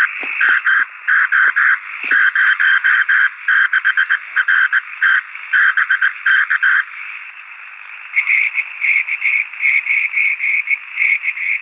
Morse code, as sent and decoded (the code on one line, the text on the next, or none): none